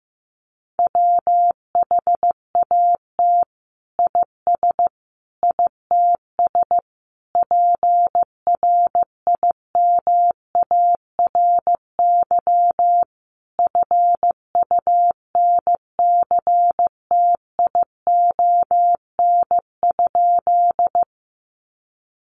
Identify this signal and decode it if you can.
Morse code, heard as a keyed tone, about 15 wpm, 700 Hz. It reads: WHAT IS ITS PRIMARY FUNCTION?
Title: Sentences from Top 1100 Words 15wpm